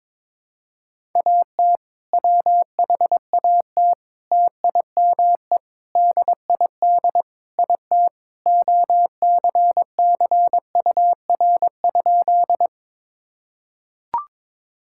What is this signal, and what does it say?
Morse code, heard as a keyed tone, about 22 wpm, 700 Hz. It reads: AT WHAT TIME DID IT OCCUR?
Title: Sentences from Top 1000 Words - Review Part 6 - 22wpm